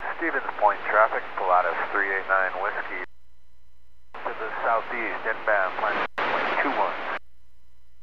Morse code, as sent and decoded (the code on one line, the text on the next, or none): none